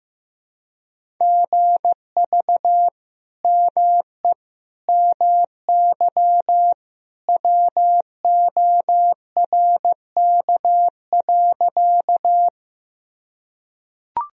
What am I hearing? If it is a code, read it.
Morse code, heard as a keyed tone, about 15 wpm, 700 Hz. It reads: GV ME MY WORK.
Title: Sentences from Top 200 Words and Abbreviations 15wpm